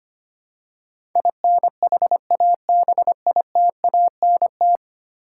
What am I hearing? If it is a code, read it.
Morse code, heard as a keyed tone, about 25 wpm, 700 Hz. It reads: INHABITANT
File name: Ten Letter Words - 25wpm